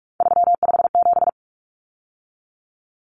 Morse code, heard as a keyed tone, about 44 wpm, 700 Hz. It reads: 357
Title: Common Cut Numbers - 3-Digit 45wpm